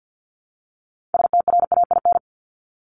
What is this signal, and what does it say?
Morse code, heard as a keyed tone, about 50 wpm, 700 Hz. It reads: STRAIN
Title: Top 1751-2000 Words 50wpm